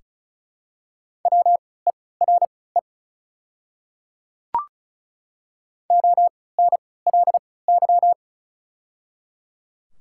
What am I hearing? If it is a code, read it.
Morse code, heard as a keyed tone, about 36 wpm, spaced as for 20 wpm, 700 Hz. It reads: WERE ONLY
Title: 3x Character Spacing - Top 500 Words - 35wpm